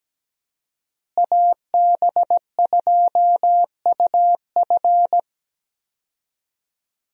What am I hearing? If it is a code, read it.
Morse code, heard as a keyed tone, about 17 wpm, 700 Hz. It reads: AB2UF
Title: USA Call Signs - 5 Characters - Repeated 2x - 17wpm